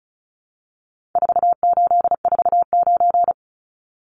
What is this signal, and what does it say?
Morse code, heard as a keyed tone, about 34 wpm, 700 Hz. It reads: 4849
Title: Numbers - 4-Digit 35wpm